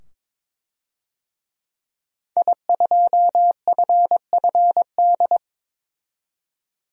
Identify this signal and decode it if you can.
Morse code, heard as a keyed tone, about 22 wpm, 700 Hz. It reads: I2FFD
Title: DXCC-100 Call Signs - 5 Characters - 22wpm